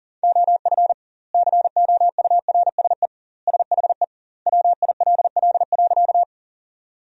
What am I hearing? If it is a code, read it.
Morse code, heard as a keyed tone, about 40 wpm, 700 Hz. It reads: OF COURSE SHE WILL.
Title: 2x Word-Spacing - Sentences from Top 500 Words - 40wpm